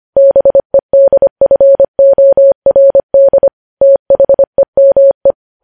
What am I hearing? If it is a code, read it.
Morse code, heard as a keyed tone, about 25 wpm, 550 Hz. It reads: BEDFORD THEME
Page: News Headlines in Morse Code at 25 WPM